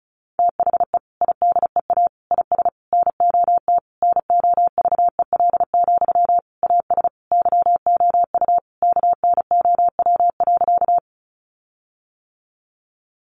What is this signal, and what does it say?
Morse code, heard as a keyed tone, about 34 wpm, 700 Hz. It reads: THE IDEA IS NOT NOVEL, AS YOU KNOW.
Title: Sentences from Top 1300 Words 35wpm